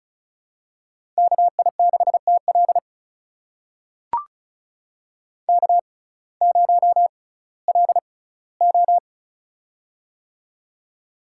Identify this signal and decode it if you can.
Morse code, heard as a keyed tone, about 36 wpm, spaced as for 19 wpm, 700 Hz. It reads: KI6TL K0LO
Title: USA Call Signs - 6x Character Spacing - 35wpm